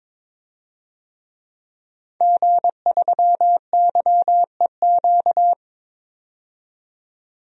Morse code, heard as a keyed tone, about 21 wpm, 700 Hz. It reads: G3YEQ
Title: DXCC-100 Call Signs - 5 Characters - 22wpm